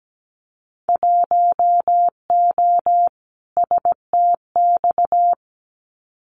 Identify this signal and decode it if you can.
Morse code, heard as a keyed tone, about 17 wpm, 700 Hz. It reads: 1O STX
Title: Winter Field Day Exchanges 17wpm